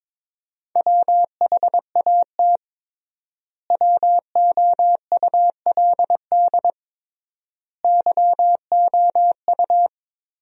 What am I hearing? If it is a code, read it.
Morse code, heard as a keyed tone, about 22 wpm, 700 Hz. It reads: WHAT WOULD YOU
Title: 3x Word-Spacing - Sentences from Top 200 Words - 22wpm